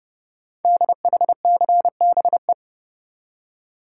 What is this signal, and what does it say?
Morse code, heard as a keyed tone, about 30 wpm, 700 Hz. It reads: DHCBE
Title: Code Groups - 5 Letters - 30wpm